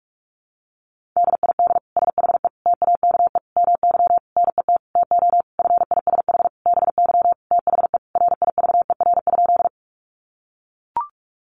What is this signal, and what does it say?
Morse code, heard as a keyed tone, about 44 wpm, 700 Hz. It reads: DID SHE TAKE MY NET TO FISH BY THE RIVER?
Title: Sentences from Top 1400 Words 45wpm